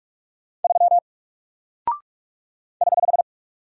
Morse code, heard as a keyed tone, about 44 wpm, 700 Hz. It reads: 3 <HH>
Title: Single Character (Letter:Number:Punctuation:Prosign) Alt 45wpm